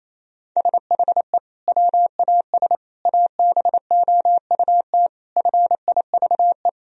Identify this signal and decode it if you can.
Morse code, heard as a keyed tone, about 28 wpm, 700 Hz. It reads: SHE WAS ABOUT FIVE
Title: Sentences from Top 1000 Words - Review Part 2 - 28wpm